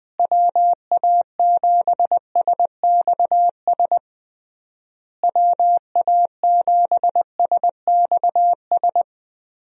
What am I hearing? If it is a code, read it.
Morse code, heard as a keyed tone, about 20 wpm, 700 Hz. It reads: WA7SXS WA7SXS
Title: USA Call Signs - 6 Characters - Repeated 3x - 20wpm